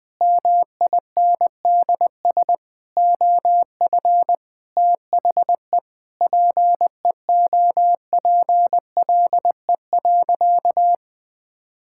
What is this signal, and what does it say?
Morse code, heard as a keyed tone, about 20 wpm, 700 Hz. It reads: MINDS OF THE PEOPLE.
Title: Sentences with 8-Word Phrases - Spoken Beginning - 20wpm